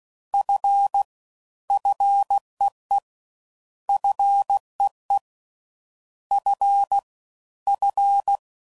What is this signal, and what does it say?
Morse code, heard as a keyed tone, about 16 wpm, 800 Hz. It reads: F FEE FEE F F